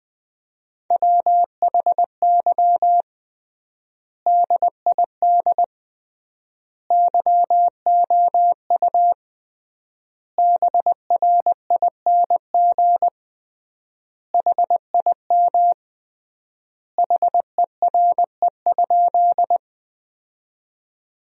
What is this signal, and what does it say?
Morse code, heard as a keyed tone, about 21 wpm, 700 Hz. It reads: WHY DID YOU BRING HIM HERE?
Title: 3x Word-Spacing - Sentences from Top 500 Words - 20wpm